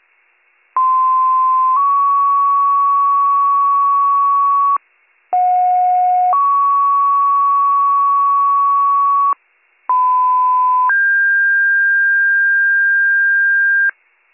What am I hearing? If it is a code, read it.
Two-tone page, heard as a one-second tone, then a three-second tone, about 1040 then 1120 Hz, about 730 then 1120 Hz, about 1010 then 1600 Hz. It is heard from 0.8 to 13.9 s.